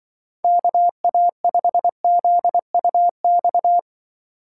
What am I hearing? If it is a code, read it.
Morse code, heard as a keyed tone, about 24 wpm, 700 Hz. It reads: KA5ZUX
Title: Speed Racing - USA 2x3 Call Signs 20wpm